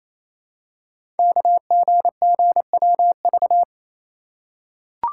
Morse code, heard as a keyed tone, about 29 wpm, 700 Hz. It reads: KGGWV
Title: Code Groups - 5 Letters - 28wpm